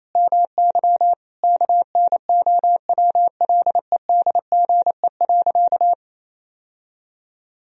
Morse code, heard as a keyed tone, about 28 wpm, 700 Hz. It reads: MY KNOWLEDGE.
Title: Sentences from Top 1000 Words - Review Part 3 - 28wpm